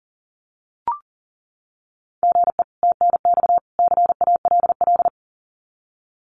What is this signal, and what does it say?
Morse code, heard as a keyed tone, about 40 wpm, 700 Hz. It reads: GE TNX CALL
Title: Common QSO Phrases 40wpm